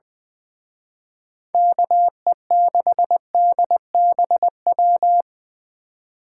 Morse code, heard as a keyed tone, about 24 wpm, 700 Hz. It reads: KE6DBW
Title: Speed Racing - USA 2x3 Call Signs 20wpm